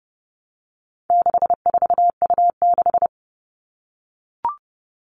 Morse code, heard as a keyed tone, about 30 wpm, 700 Hz. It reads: 64U6
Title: All Cut Numbers - 4-Digit 30wpm